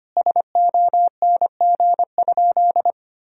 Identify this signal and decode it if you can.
Morse code, heard as a keyed tone, about 25 wpm, 700 Hz. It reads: SONG?
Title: Sentences from Top 1000 Words - Review Part 2 - 25wpm